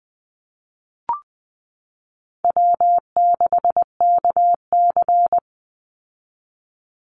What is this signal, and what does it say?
Morse code, heard as a keyed tone, about 20 wpm, 700 Hz. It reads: W6KC
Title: Easy Call Signs Repeated 3x - 20wpm